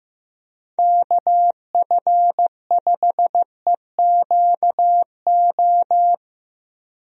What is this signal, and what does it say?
Morse code, heard as a keyed tone, about 15 wpm, 700 Hz. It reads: KF5EQO
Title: USA Call Signs - 6 Characters - Repeated 2x - 15wpm